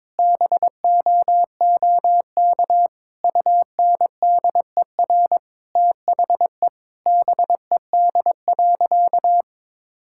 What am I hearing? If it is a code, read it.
Morse code, heard as a keyed tone, about 22 wpm, 700 Hz. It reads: BOOK UNDER THE BED.
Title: Sentences from Top 1000 Words - Review Part 6 - 22wpm